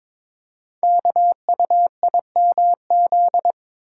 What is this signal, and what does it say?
Morse code, heard as a keyed tone, about 22 wpm, 700 Hz. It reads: KUIMZ
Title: Code Groups - 5 Letters - 22wpm